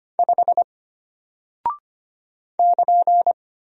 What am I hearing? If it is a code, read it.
Morse code, heard as a keyed tone, about 25 wpm, 700 Hz. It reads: 5 <KN>
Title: Single Character (Letter:Number:Punctuation:Prosign) Alt 25wpm